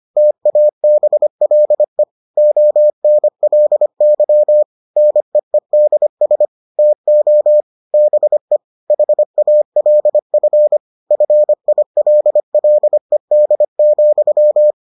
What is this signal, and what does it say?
Morse code, heard as a keyed tone, about 25 wpm, 600 Hz. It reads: TABLE ONLY NEEDS TO BE HALF FILLED,